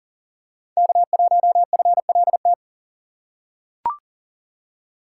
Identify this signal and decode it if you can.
Morse code, heard as a keyed tone, about 40 wpm, 700 Hz. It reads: K1FLT
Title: USA Call Signs - 5 Characters - 2x Character Spacing - 40wpm